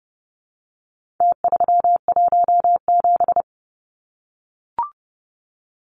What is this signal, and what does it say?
Morse code, heard as a keyed tone, about 30 wpm, 700 Hz. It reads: T317
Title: Numbers - Contest Style - 4-Digit 30wpm